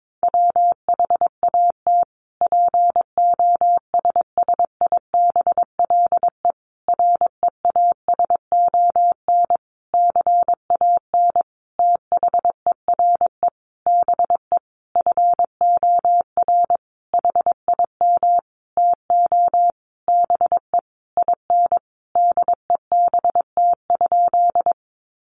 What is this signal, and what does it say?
Morse code, heard as a keyed tone, about 22 wpm, 700 Hz. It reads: WHAT POSSIBLE REASON CAN THERE BE FOR HIM TO BE IN DEBT?
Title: Sentences from Top 1300 Words 22wpm